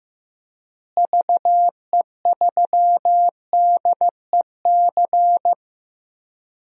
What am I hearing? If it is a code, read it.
Morse code, heard as a keyed tone, about 15 wpm, 700 Hz. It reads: VE3DEC
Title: DXCC-100 Call Signs - 6 Characters - Repeated 3x - 15wpm